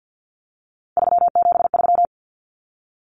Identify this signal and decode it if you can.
Morse code, heard as a keyed tone, about 50 wpm, 700 Hz. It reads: 373
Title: Numbers - 3-Digit 50wpm